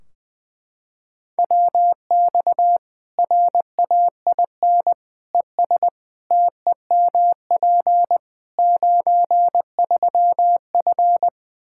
Morse code, heard as a keyed tone, about 20 wpm, 700 Hz. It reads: WX RAIN ES TEMP 93F
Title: Practice QSO - Exchange 2 - Info - 20wpm